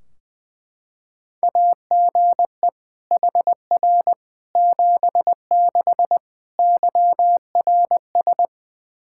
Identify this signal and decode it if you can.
Morse code, heard as a keyed tone, about 21 wpm, 700 Hz. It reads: AGE HR 76 YRS
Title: Practice QSO - Exchange 3 - More Info - 20wpm